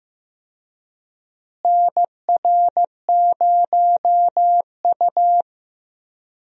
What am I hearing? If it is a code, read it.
Morse code, heard as a keyed tone, about 15 wpm, 700 Hz. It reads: NR0U
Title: USA Call Signs - 4 Characters - Repeated 3x - 15wpm